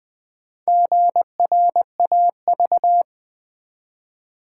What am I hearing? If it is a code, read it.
Morse code, heard as a keyed tone, about 20 wpm, 700 Hz. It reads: GRAV
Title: Build A Word - 1 Letter At A Time - 5-Letter Words - 20wpm